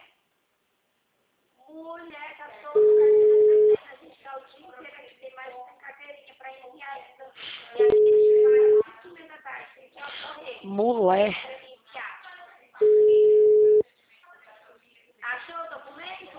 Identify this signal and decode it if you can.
Ringback tone (Europe, 425 Hz), with speech.